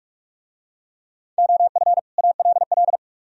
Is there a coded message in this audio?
Yes: Morse code from 1.4 to 3.0 s, heard as a keyed tone, about 42 wpm, 700 Hz. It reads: OF ALL